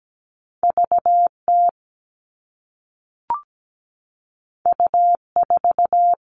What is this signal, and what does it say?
Morse code, heard as a keyed tone, about 17 wpm, 700 Hz. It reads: VT U4
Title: All Cut Numbers - 2-Digit 17wpm